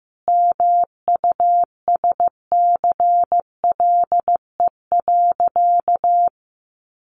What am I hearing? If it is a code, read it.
Morse code, heard as a keyed tone, about 15 wpm, 700 Hz. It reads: MUSCLE.
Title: Sentences from Top 1300 Words 15wpm